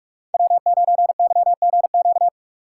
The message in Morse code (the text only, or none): W9YGQ